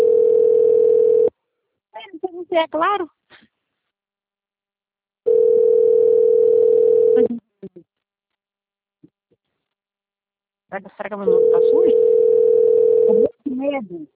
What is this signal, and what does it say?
Ringback tone (North America), 440 + 480 Hz, with speech.
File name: out- - - - - .28046